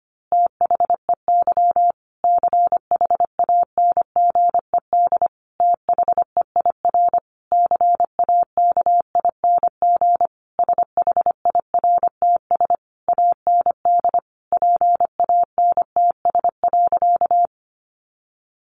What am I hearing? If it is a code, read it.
Morse code, heard as a keyed tone, about 25 wpm, 700 Hz. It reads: THEY CHANGED THEIR CAKING SHIRTS AND PANTS.
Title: A Sound Of Thunder 25wpm